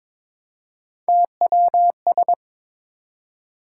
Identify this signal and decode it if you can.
Morse code, heard as a keyed tone, about 21 wpm, 700 Hz. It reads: TWS
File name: practice / 22wpm / Three Letter-Number Combinations - 22wpm